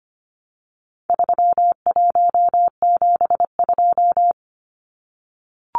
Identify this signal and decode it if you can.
Morse code, heard as a keyed tone, about 25 wpm, 700 Hz. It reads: 3172
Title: Numbers - 4-Digit 25wpm